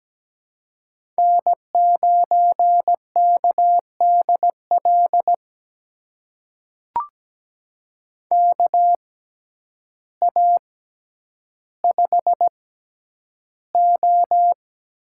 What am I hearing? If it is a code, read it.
Morse code, heard as a keyed tone, about 17 wpm, spaced as for 9 wpm, 700 Hz. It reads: N9KDL KA5O
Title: USA Call Signs - 6x Character Spacing - 17wpm